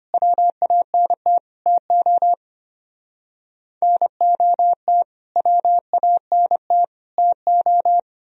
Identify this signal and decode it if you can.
Morse code, heard as a keyed tone, about 30 wpm, 700 Hz. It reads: WANT TO NOT WANT TO
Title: Speed-Racing - Sets of 3 Words 25wpm